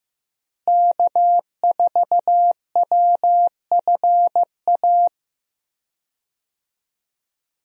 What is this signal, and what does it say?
Morse code, heard as a keyed tone, about 16 wpm, 700 Hz. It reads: K4WFA
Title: USA Call Signs - 5 Characters - Repeated 2x - 15wpm